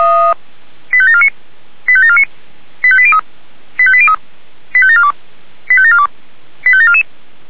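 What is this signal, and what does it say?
Five-tone selective call (ZVEI), heard as quick runs of five tones, tones of about 70 ms. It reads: 85749 85749 85803 85803 85742 85742 85744
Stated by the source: Five-tone ZVEI selective call received by SDR.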